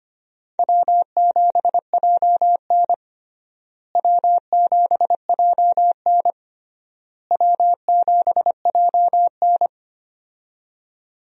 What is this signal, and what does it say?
Morse code, heard as a keyed tone, about 25 wpm, 700 Hz. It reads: W7JN W7JN W7JN
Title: USA Call Signs - 4 Characters - Repeated 3x - 25wpm